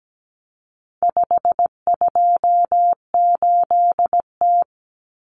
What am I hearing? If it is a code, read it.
Morse code, heard as a keyed tone, about 17 wpm, 700 Hz. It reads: 528T
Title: Common Cut Numbers - 4-Digit 17wpm